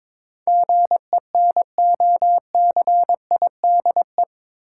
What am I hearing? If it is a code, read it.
Morse code, heard as a keyed tone, about 22 wpm, 700 Hz. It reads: GENOCIDE
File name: practice / 22wpm / Eight Letter Words - 22wpm